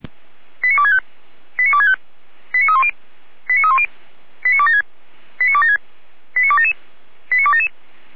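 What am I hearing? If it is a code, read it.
Five-tone selective call (ZVEI), heard as quick runs of five tones, tones of about 70 ms. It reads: 89265 89265 89310 89310 89276 89276 89277 89277
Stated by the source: ZVEI five-tone call 89337 received by SDR, from audio.f85360.feuerwehr-orlamuende.org, fire brigade Orlamünde.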